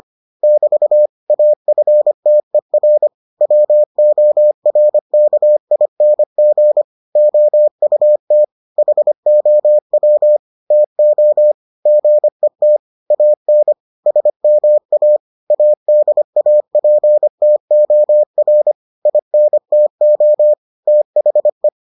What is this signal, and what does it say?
Morse code, heard as a keyed tone, about 25 wpm, 600 Hz. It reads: <BT> AFTER WORKING OUT HOW TO GET AN SMA ADAPTOR INTO THE